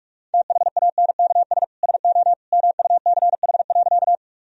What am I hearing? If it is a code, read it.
Morse code, heard as a keyed tone, about 46 wpm, 700 Hz. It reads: THANKS SO MUCH.
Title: Sentences from Top 1000 Words - Review Part 3 - 45wpm